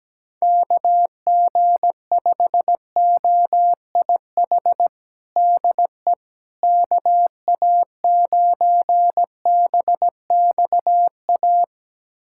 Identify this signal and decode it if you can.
Morse code, heard as a keyed tone, about 17 wpm, 700 Hz. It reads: KG5OIH DE KA9BXA
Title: USA Callsign DE Callsign - 6 Characters - 17wpm